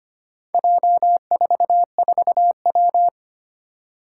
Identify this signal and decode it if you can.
Morse code, heard as a keyed tone, about 25 wpm, 700 Hz. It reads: J44W
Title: Four Challenging Letter-Number Combinations 25wpm